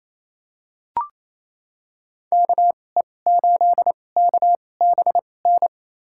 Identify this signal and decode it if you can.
Morse code, heard as a keyed tone, about 28 wpm, spaced as for 20 wpm, 700 Hz. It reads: KE8KBN
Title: USA Call Signs - 6 Characters - 2x Character Spacing - 28wpm